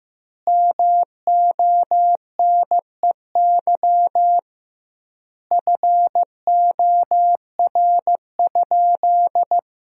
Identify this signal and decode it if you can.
Morse code, heard as a keyed tone, about 15 wpm, 700 Hz. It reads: MONEY FOR?
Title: 2x Word-Spacing - Sentences from Top 400 Words - 15wpm